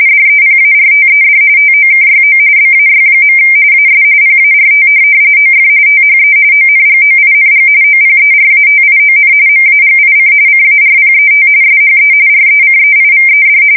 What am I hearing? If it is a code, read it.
RTTY, about 45.45 baud. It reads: DEPARTED TOKYO BAY. AS PART OF THE ONGOING OPERATION MAGIC CARPET SHE RECEIVED
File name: fsk170 / USS-Missouri